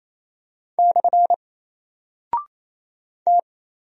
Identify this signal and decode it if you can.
Morse code, heard as a keyed tone, about 28 wpm, 700 Hz. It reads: / T
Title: Single Character (Letter:Number:Punctuation:Prosign) Alt 28wpm